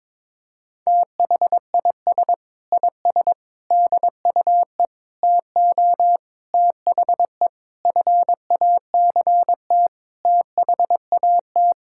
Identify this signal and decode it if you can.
Morse code, heard as a keyed tone, about 22 wpm, 700 Hz. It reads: THIS IS DUE TO THE FACT THAT
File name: practice / 22wpm / Sets of 8 Words - 22wpm